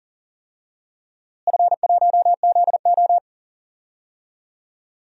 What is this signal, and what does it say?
Morse code, heard as a keyed tone, about 37 wpm, 700 Hz. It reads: F1ZO
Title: DXCC-100 Call Signs - 4 Characters - Repeated 2x - 40wpm